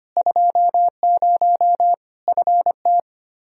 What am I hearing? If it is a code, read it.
Morse code, heard as a keyed tone, about 25 wpm, 700 Hz. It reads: 20 FT
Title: Practice QSO - Exchange 2 - Info - 25wpm